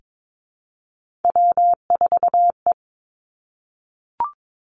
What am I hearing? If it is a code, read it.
Morse code, heard as a keyed tone, about 22 wpm, 700 Hz. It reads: W4E
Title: Three Challenging Letter-Number Combinations 22wpm